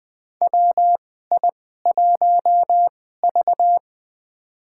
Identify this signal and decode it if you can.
Morse code, heard as a keyed tone, about 20 wpm, spaced as for 15 wpm, 700 Hz. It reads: WI1V
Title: USA Call Signs - 4 Characters - 2x Character Spacing - 20wpm